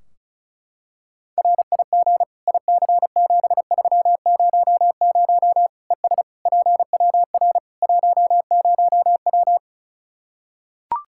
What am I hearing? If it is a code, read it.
Morse code, heard as a keyed tone, about 36 wpm, 700 Hz. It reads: RIG IC7300 ES PWR 10W
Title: K1USN SST QSO Practice 35wpm